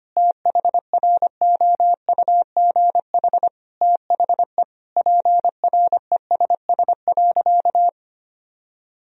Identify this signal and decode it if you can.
Morse code, heard as a keyed tone, about 25 wpm, 700 Hz. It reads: THROUGH THE PRESS.
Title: Sentences from Top 1000 Words - Review Part 1 - 25wpm